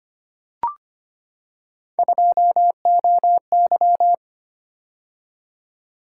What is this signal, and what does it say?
Morse code, heard as a keyed tone, about 25 wpm, 700 Hz. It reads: 2OY
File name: Three Letter-Number Combinations - 25wpm